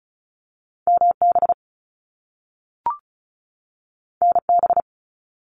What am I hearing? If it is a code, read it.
Morse code, heard as a keyed tone, about 36 wpm, 700 Hz. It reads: MB NB
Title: Canadian Provinces and Territories 35wpm